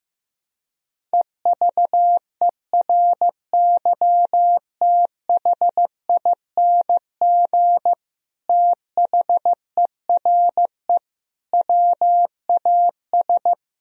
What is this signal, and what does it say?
Morse code, heard as a keyed tone, about 15 wpm, 700 Hz. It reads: EVERYTHING THERE WAS
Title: Sentences from Top 1000 Words - Review Part 2 - 15wpm